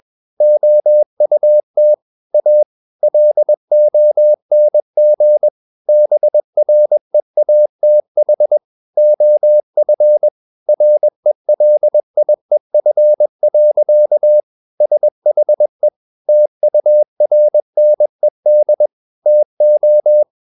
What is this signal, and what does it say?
Morse code, heard as a keyed tone, about 21 wpm, 600 Hz. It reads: OUT A LONG BREATH OF RELIEF. SHE TURNED TO
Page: Romanen Never af Ken Follet på engelsk.